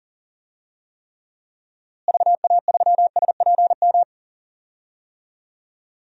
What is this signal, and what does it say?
Morse code, heard as a keyed tone, about 40 wpm, 700 Hz. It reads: VA3SPM
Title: DXCC-100 Call Signs - 6 Characters - Repeated 3x - 40wpm